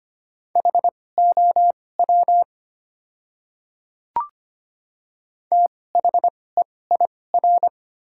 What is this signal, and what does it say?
Morse code, heard as a keyed tone, about 25 wpm, spaced as for 18 wpm, 700 Hz. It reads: HOW THEIR
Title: 2x Character Spacing - Top 100 Words - 25wpm